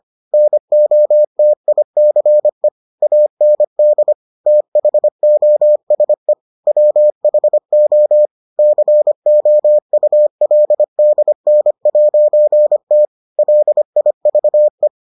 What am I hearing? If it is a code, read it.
Morse code, heard as a keyed tone, about 25 wpm, 600 Hz. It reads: NOTICE AND THOSE WHO COULDN'T LIVE